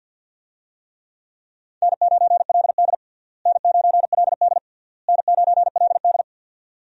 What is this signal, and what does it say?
Morse code, heard as a keyed tone, about 50 wpm, 700 Hz. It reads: N9LD N9LD N9LD
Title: USA Call Signs - 4 Characters - Repeated 3x - 50wpm